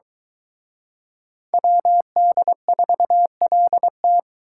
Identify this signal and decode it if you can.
Morse code, heard as a keyed tone, about 24 wpm, 700 Hz. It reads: WD4LT